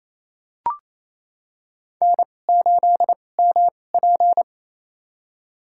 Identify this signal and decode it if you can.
Morse code, heard as a keyed tone, about 28 wpm, spaced as for 20 wpm, 700 Hz. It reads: N8MP
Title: USA Call Signs - 4 Characters - 2x Character Spacing - 28wpm